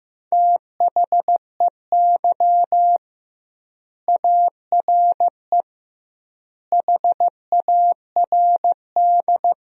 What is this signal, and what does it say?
Morse code, heard as a keyed tone, about 15 wpm, 700 Hz. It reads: THEY ARE HARD
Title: 2x Word-Spacing - Sentences from Top 300 Words - 15wpm